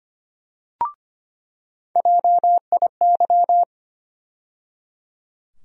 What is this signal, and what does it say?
Morse code, heard as a keyed tone, about 25 wpm, 700 Hz. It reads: JIY
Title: USA Call Sign Postfixes - 3 Letters 25wpm